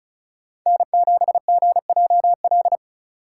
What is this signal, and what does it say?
Morse code, heard as a keyed tone, about 34 wpm, 700 Hz. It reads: N7GJL
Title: USA Call Signs - 5 Characters - 3x Character Spacing - 35wpm